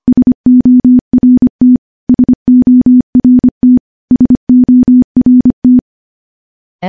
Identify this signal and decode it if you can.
Morse code, heard as a keyed tone, about 25 wpm, 250 Hz. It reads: SORT SORT SORT
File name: morse / morse-training-25WPM